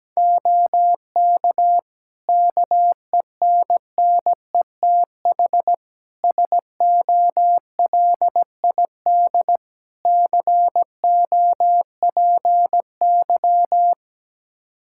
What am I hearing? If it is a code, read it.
Morse code, heard as a keyed tone, about 17 wpm, 700 Hz. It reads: OK KENNETH SOLID COPY
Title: Practice QSO - Exchange 3 - More Info - 17wpm